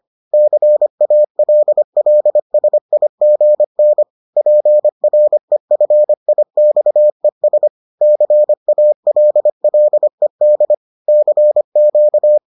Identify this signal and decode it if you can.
Morse code, heard as a keyed tone, about 25 wpm, 600 Hz. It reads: CALLSIGN PREFIXES CALLED CQ